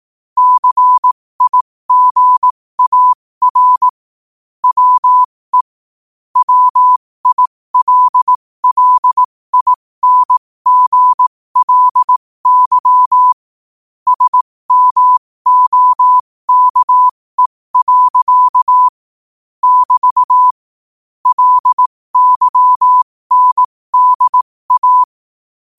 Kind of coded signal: Morse code